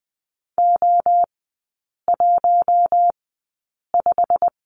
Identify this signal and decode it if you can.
Morse code, heard as a keyed tone, about 20 wpm, spaced as for 8 wpm, 690 Hz. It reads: O15
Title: 133 Lesson 22e - Question Mark - Three Characters 20wpm